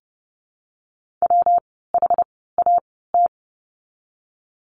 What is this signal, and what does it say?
Morse code, heard as a keyed tone, about 30 wpm, spaced as for 17 wpm, 700 Hz. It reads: WHAT
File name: course / 30:10wpm / 115 Lesson 19f - Letter w - Words 30wpm